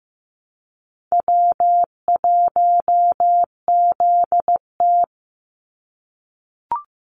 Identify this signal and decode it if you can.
Morse code, heard as a keyed tone, about 15 wpm, 700 Hz. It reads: W1ZT
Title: Easy Call Signs - 15wpm